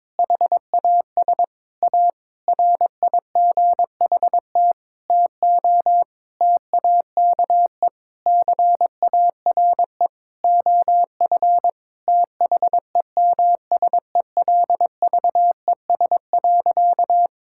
Morse code, heard as a keyed tone, about 22 wpm, 700 Hz. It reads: HAS A RIGHT TO TAKE CARE OF THEMSELVES.
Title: Sentences from Top 1000 Words - Review Part 5 - 22wpm